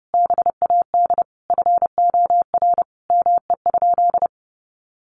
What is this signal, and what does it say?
Morse code, heard as a keyed tone, about 30 wpm, 700 Hz. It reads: BAD FOR ME?
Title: Sentences from Top 700 Words 30wpm